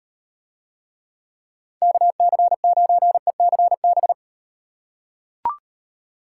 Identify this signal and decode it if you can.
Morse code, heard as a keyed tone, about 38 wpm, 700 Hz. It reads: KC9ECB